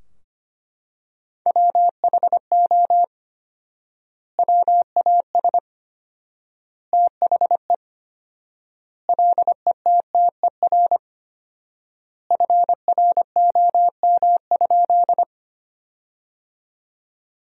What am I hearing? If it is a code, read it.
Morse code, heard as a keyed tone, about 25 wpm, 700 Hz. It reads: WHO WAS THE LETTER FROM?
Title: 4x Word-Spacing - Sentences from Top 300 Words - 25wpm